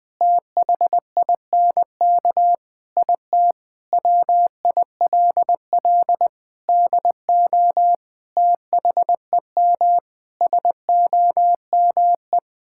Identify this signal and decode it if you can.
Morse code, heard as a keyed tone, about 20 wpm, 700 Hz. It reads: THINK IT WILL DO THEM SOME
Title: Sentences from Top 1000 Words - Review Part 6 - 20wpm